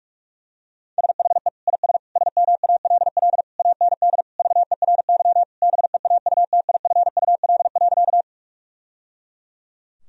Morse code, heard as a keyed tone, about 41 wpm, 700 Hz. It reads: SHE IS SMALL AND VERY BEAUTIFUL.
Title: Sentences from Top 1000 Words - Review Part 1 - 45wpm